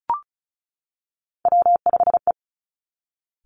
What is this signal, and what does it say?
Morse code, heard as a keyed tone, about 36 wpm, 700 Hz. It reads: W5E
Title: Three Challenging Letter-Number Combinations 35wpm